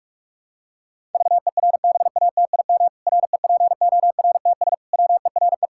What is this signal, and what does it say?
Morse code, heard as a keyed tone, about 46 wpm, 700 Hz. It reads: VERBATIM REPORTS WERE
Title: Pangram Challenge v2 - 45wpm